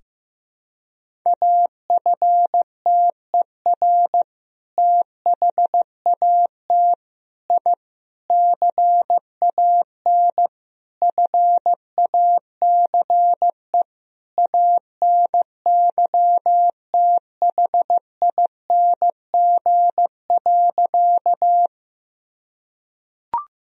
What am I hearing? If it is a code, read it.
Morse code, heard as a keyed tone, about 15 wpm, 700 Hz. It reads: AFTER THAT I CAN FACE ANYTHING.
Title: Sentences from Top 1000 Words - Review Part 2 - 15wpm